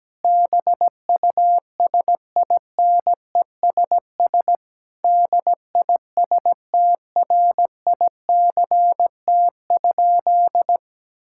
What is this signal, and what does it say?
Morse code, heard as a keyed tone, about 17 wpm, 700 Hz. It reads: BUSINESS DISTRICT?
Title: Sentences from Top 1600 Words 17wpm